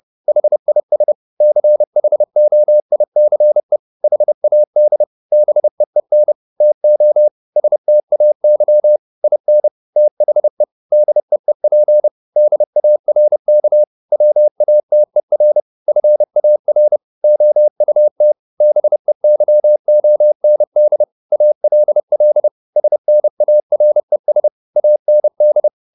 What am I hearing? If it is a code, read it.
Morse code, heard as a keyed tone, about 30 wpm, 600 Hz. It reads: HIS CHOICE HAD BEEN TO STAY IN THE DEEP DARK WATER FAR OUT BEYOND ALL SNARES AND
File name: the old man and the sea / the old man and the sea 1 30wpm